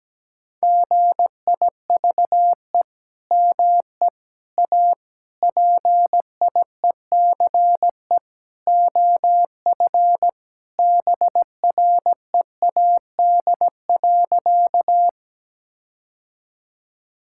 Morse code, heard as a keyed tone, about 17 wpm, 700 Hz. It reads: GIVE ME A PIECE OF BREAD.
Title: Sentences from Top 1600 Words 17wpm